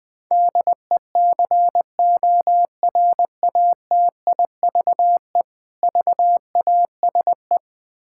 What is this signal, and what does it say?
Morse code, heard as a keyed tone, about 21 wpm, 700 Hz. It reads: DECORATIVE VASE
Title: Difficult Words with Hints - 20wpm